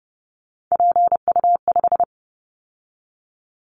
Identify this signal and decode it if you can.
Morse code, heard as a keyed tone, about 30 wpm, 700 Hz. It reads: PU5
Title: Callsign Prefixes for Top 50 DXCC - 30wpm